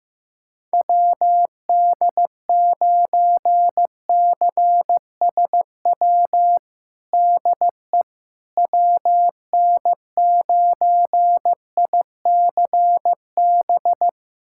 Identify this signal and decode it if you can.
Morse code, heard as a keyed tone, about 15 wpm, 700 Hz. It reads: WD9CSW DE WN9ICB
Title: USA Callsign DE Callsign 15wpm